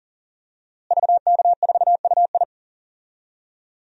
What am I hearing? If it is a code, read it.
Morse code, heard as a keyed tone, about 40 wpm, 700 Hz. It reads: VK4UI
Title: DXCC-100 Call Signs - 5 Characters - Repeated 2x - 40wpm